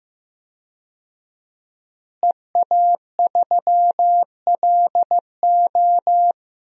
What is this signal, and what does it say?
Morse code, heard as a keyed tone, about 15 wpm, 700 Hz. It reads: EA3LO
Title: DXCC-100 Call Signs - 5 Characters - Repeated 3x - 15wpm